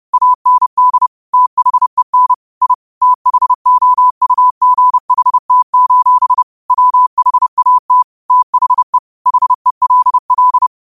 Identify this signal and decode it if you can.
Morse code, heard as a keyed tone, about 30 wpm, 1000 Hz. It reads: AND THEN I THOUGHT: WHAT THE HELL
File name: qotdfiles / 30wpm